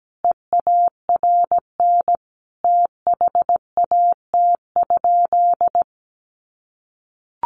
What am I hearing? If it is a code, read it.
Morse code, heard as a keyed tone, about 17 wpm, 700 Hz. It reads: EARN THAT?
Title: Sentences from Top 1500 Words 17wpm